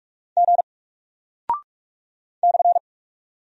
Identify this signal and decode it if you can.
Morse code, heard as a keyed tone, about 44 wpm, 700 Hz. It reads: G /
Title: Single Character (Letter:Number:Punctuation:Prosign) 45wpm